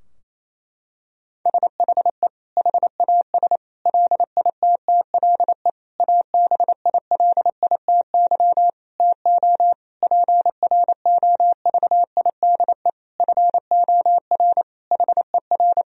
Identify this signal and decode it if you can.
Morse code, heard as a keyed tone, about 28 wpm, 700 Hz. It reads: SHE HAS LITTLE ABILITY TO PROVIDE FOR HER
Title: Sentences from Top 1000 Words - Review Part 3 - 28wpm